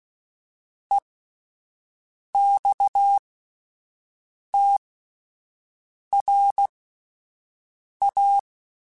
Morse code, heard as a keyed tone, about 16 wpm, spaced as for 5 wpm, 800 Hz. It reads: EXTRA
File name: fromSMs / FromSMs mp3s / Classes@5